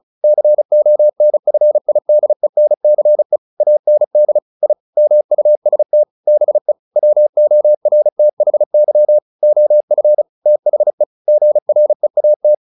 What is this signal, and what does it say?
Morse code, heard as a keyed tone, about 34 wpm, 600 Hz. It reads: CONFIDENCE AND I MUST BE WORTHY OF THE GREAT
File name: the old man and the sea / the old man and the sea 1 35wpm